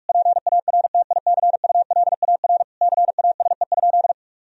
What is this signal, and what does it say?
Morse code, heard as a keyed tone, about 45 wpm, 700 Hz. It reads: PARTICULAR CASE?
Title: Sentences from Top 1000 Words - Review Part 4 - 45wpm